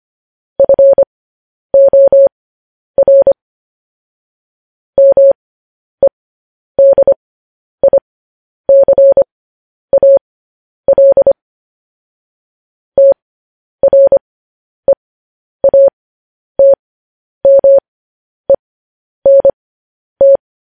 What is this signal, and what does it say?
Morse code, heard as a keyed tone, about 25 wpm, spaced as for 10 wpm, 550 Hz. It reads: FOR MEDICAL TREATMENT
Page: news-headlines-in-morse-code-at-10-wpm